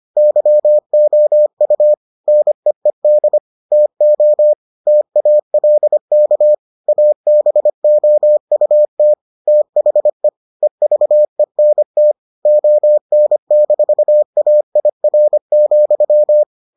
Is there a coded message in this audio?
Yes: Morse code from 0.2 to 16.4 s, heard as a keyed tone, about 25 wpm, 600 Hz. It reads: YOU NEED TO TALK ABOUT THE EVENT ON-AIR,